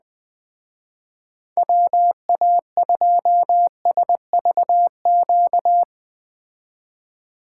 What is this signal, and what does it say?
Morse code, heard as a keyed tone, about 21 wpm, 700 Hz. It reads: WA2SVQ